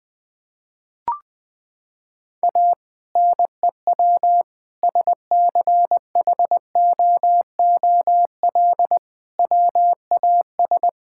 Morse code, heard as a keyed tone, about 20 wpm, 700 Hz. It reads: A NEW SCHOOL WAS
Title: Sentences with 2-Word Phrases - 20wpm